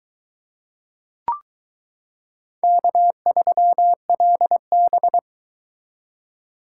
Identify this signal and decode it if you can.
Morse code, heard as a keyed tone, about 22 wpm, 700 Hz. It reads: K3LB